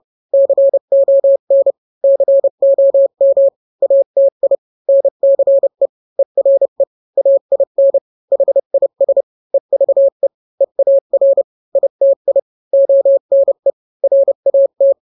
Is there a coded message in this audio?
Yes: Morse code from 0.3 to 14.9 s, heard as a keyed tone, about 30 wpm, 550 Hz. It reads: CON COM ATI NCE ERE AIN HIS EVE EAR ITI ONE RAT